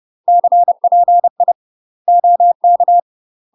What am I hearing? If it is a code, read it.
Morse code, heard as a keyed tone, about 30 wpm, 700 Hz. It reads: CPI OK
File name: UK MASTER QSOs TRAINING / 30WPM / UK MASTER QSOs 0050